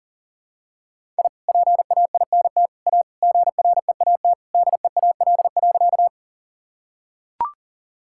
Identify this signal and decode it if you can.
Morse code, heard as a keyed tone, about 40 wpm, 700 Hz. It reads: I PAINT A GREAT DEAL.
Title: Sentences from Top 1000 Words - Review Part 6 - 40wpm